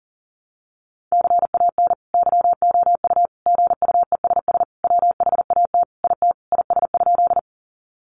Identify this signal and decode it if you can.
Morse code, heard as a keyed tone, about 37 wpm, 700 Hz. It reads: CAN YOU GUESS WHAT IT IS?
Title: Sentences from Top 1500 Words 40wpm